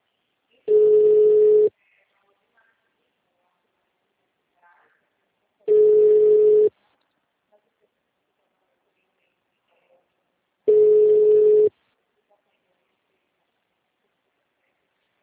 Ringback tone (Europe, 425 Hz).